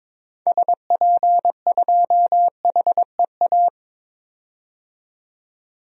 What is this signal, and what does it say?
Morse code, heard as a keyed tone, about 22 wpm, 700 Hz. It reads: SP2HEA
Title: DXCC-100 Call Signs - 6 Characters - 22wpm